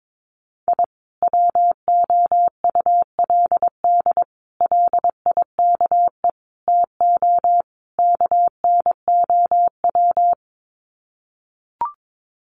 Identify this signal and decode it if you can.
Morse code, heard as a keyed tone, about 22 wpm, 700 Hz. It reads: I WOULD LIKE TO KNOW
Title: Sets of 5 Words Encore 22wpm